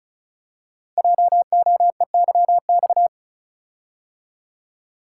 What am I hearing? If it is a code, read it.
Morse code, heard as a keyed tone, about 36 wpm, 700 Hz. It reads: JOEYX
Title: Code Groups - 5 Letter:Numbers - 35wpm